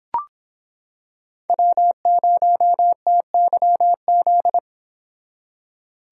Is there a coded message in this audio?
Yes: Morse code from 1.5 to 4.6 s, heard as a keyed tone, about 26 wpm, 700 Hz. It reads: W0TYZ